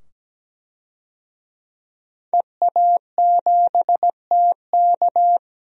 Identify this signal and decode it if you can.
Morse code, heard as a keyed tone, about 17 wpm, 700 Hz. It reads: EA7TK
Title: DXCC-100 Call Signs - 5 Characters - 17wpm